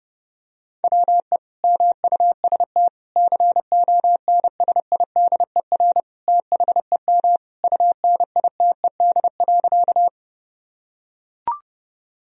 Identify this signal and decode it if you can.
Morse code, heard as a keyed tone, about 31 wpm, 700 Hz. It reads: WE MUST CONSIDER THEM UNITED.
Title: Sentences from Top 1000 Words - Review Part 4 - 30wpm